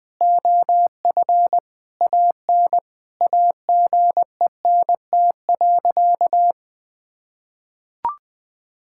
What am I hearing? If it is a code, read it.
Morse code, heard as a keyed tone, about 20 wpm, 700 Hz. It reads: OF AN AGENT.
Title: Sentences from Top 1600 Words 20wpm